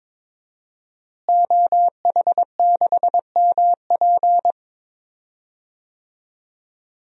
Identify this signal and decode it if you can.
Morse code, heard as a keyed tone, about 22 wpm, 700 Hz. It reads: OH6MP
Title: DXCC-100 Call Signs - 5 Characters - Repeated 3x - 22wpm